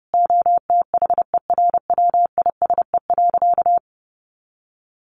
Morse code, heard as a keyed tone, about 30 wpm, 700 Hz. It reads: OTHERWISE.
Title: Sentences from Top 1200 Words 30wpm